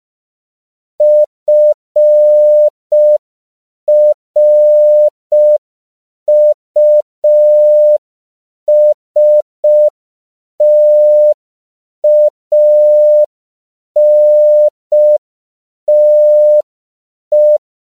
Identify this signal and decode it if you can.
Morse code, heard as a keyed tone, about 5 wpm, 600 Hz. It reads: FRUSTANTE
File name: CW / 5PPM